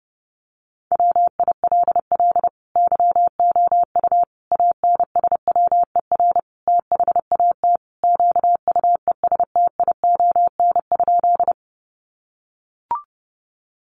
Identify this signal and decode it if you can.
Morse code, heard as a keyed tone, about 30 wpm, 700 Hz. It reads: WILL YOU ANSWER THAT QUESTION?
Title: Sentences from Top 400 Words 30wpm